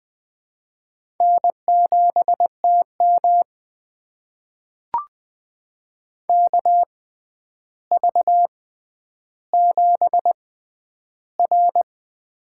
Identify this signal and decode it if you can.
Morse code, heard as a keyed tone, about 20 wpm, spaced as for 10 wpm, 700 Hz. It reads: N7TM KV7R
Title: USA Call Signs - 4 Characters - 6x Character Spacing - 20wpm